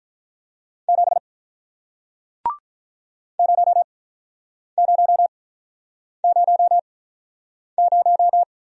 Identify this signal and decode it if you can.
Morse code, heard as a keyed tone, about 48 wpm, spaced as for 9 wpm, 700 Hz. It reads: 7 0000